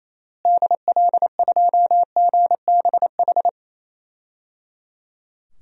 Morse code, heard as a keyed tone, about 28 wpm, 700 Hz. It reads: DL2GBH
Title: DXCC-100 Call Signs - 6 Characters - Repeated 3x - 28wpm